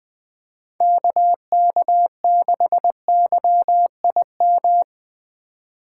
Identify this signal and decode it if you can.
Morse code, heard as a keyed tone, about 21 wpm, 700 Hz. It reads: KK6YIM